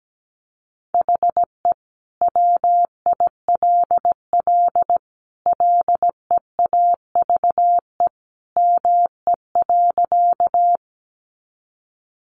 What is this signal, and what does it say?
Morse code, heard as a keyed tone, about 17 wpm, 700 Hz. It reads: HE WILL LEAVE ME.
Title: Sentences from Top 400 Words 17wpm